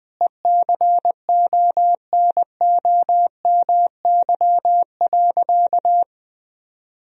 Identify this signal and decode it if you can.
Morse code, heard as a keyed tone, about 20 wpm, 700 Hz. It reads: ECONOMY.
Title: Sentences from Top 1000 Words - Review Part 3 - 20wpm